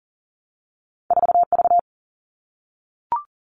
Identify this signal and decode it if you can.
Morse code, heard as a keyed tone, about 39 wpm, 700 Hz. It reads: 4 V
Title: Two Challenging Letter-Number Combinations 40wpm